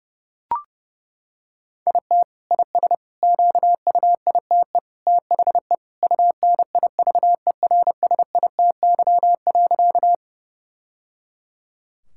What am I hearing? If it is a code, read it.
Morse code, heard as a keyed tone, about 30 wpm, 700 Hz. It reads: IT IS QUITE THE UNIVERSITY.
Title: Sentences from Top 1000 Words - Review Part 5 - 30wpm